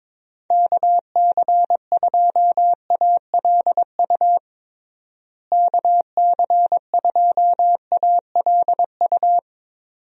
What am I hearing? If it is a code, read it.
Morse code, heard as a keyed tone, about 22 wpm, 700 Hz. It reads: KC2ALU KC2ALU
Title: USA Call Signs - 6 Characters - Repeated 2x - 22wpm